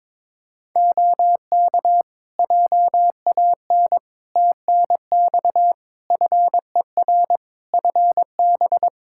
Morse code, heard as a keyed tone, about 22 wpm, 700 Hz. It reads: OK JAN TNX FER FB
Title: K1USN SST QSO Practice 22wpm